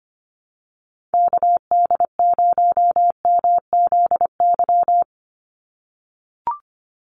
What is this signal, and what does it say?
Morse code, heard as a keyed tone, about 25 wpm, 700 Hz. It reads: KD0MZY
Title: USA Call Signs 25wpm